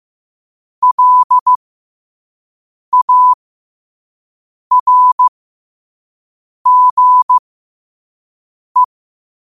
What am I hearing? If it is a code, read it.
Morse code, heard as a keyed tone, about 15 wpm, spaced as for 5 wpm, 1000 Hz. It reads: LARGE
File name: qotdfiles / 5wpm